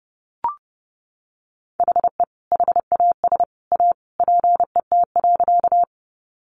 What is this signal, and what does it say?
Morse code, heard as a keyed tone, about 30 wpm, 700 Hz. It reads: HE HAS A PET.
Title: Easy Phrases 30wpm